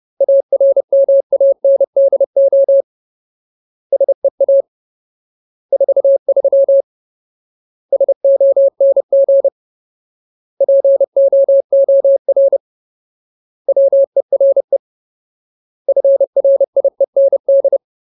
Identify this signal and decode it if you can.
Morse code, heard as a keyed tone, about 30 wpm, 550 Hz. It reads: ARMANDO SEA 43 SONG POOR WERE FRIEND
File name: Ultimate Word List / 30WPM Ultimate